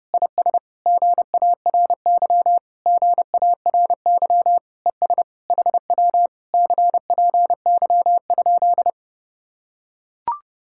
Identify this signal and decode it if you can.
Morse code, heard as a keyed tone, about 32 wpm, 700 Hz. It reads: IS GARY GARY ES HW CPY?